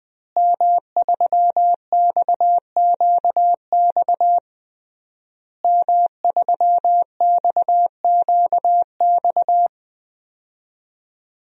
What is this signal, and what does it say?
Morse code, heard as a keyed tone, about 21 wpm, 700 Hz. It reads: M3XQX M3XQX
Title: DXCC-100 Call Signs - 5 Characters - Repeated 3x - 20wpm